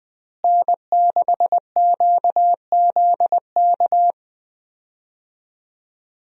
Morse code, heard as a keyed tone, about 20 wpm, 700 Hz. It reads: N6QZK
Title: USA Call Signs - 5 Characters - Repeated 3x - 20wpm